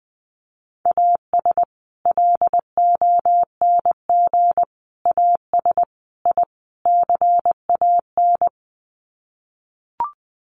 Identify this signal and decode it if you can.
Morse code, heard as a keyed tone, about 20 wpm, 700 Hz. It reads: AS LONG AS I CAN
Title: Sets of 5 Words Encore2 20wpm